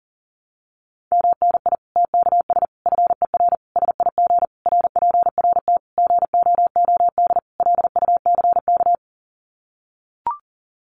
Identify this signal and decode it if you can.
Morse code, heard as a keyed tone, about 40 wpm, 700 Hz. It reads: MNI TKS FER SIG RPRT GOOD LUCK
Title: Common QSO Phrases 40wpm